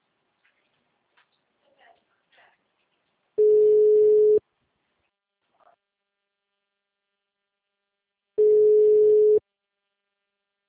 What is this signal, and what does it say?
Ringback tone (Europe, 425 Hz).